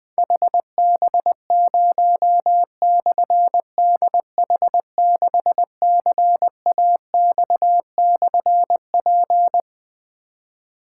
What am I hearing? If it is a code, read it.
Morse code, heard as a keyed tone, about 20 wpm, 700 Hz. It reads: HB0/DH6CAX/P
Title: Difficult Call Signs 20wpm